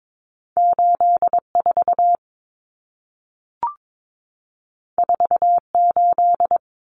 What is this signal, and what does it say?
Morse code, heard as a keyed tone, about 22 wpm, 700 Hz. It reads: 84 48
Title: Common Cut Numbers - 2-Digit 22wpm